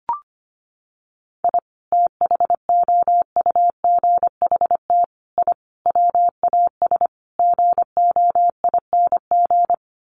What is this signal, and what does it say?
Morse code, heard as a keyed tone, about 25 wpm, 700 Hz. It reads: I THOUGHT I WAS GOING
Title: Sets of 5 Words Encore2 25wpm